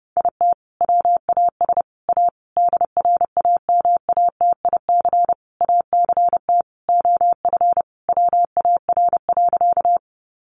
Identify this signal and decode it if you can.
Morse code, heard as a keyed tone, about 30 wpm, 700 Hz. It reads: IT WAS A DRAMATIC ACT OF WAR.
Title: Sentences from Top 1300 Words 30wpm